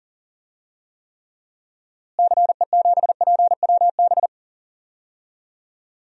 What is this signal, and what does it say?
Morse code, heard as a keyed tone, about 40 wpm, 700 Hz. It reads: CE7PWB
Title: DXCC-100 Call Signs - 6 Characters - Repeated 2x - 40wpm